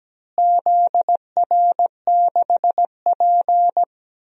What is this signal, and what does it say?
Morse code, heard as a keyed tone, about 17 wpm, 700 Hz. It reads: ZR6P
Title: DXCC-100 Call Signs - 4 Characters - Repeated 3x - 17wpm